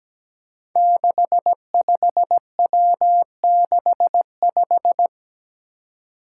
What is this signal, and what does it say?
Morse code, heard as a keyed tone, about 17 wpm, 700 Hz. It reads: 65W65
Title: Five Challenging Letter-Number Combinations 17wpm